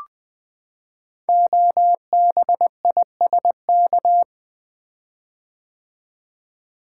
Morse code, heard as a keyed tone, about 20 wpm, 700 Hz. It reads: OBISK
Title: Code Groups - 5 Letters - 20wpm